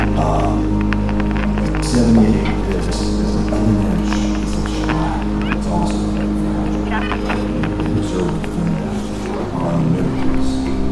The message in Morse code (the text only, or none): none